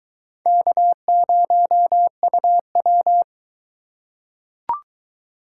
Morse code, heard as a keyed tone, about 22 wpm, 700 Hz. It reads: K0UW